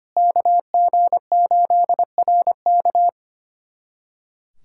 Morse code, heard as a keyed tone, about 25 wpm, 700 Hz. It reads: KG8RK
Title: USA Callsign DE Callsign 25wpm